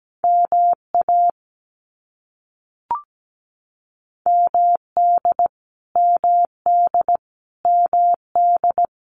Morse code, heard as a keyed tone, about 17 wpm, 700 Hz. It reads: MA MD MD MD
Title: US States Repeated 3x 17wpm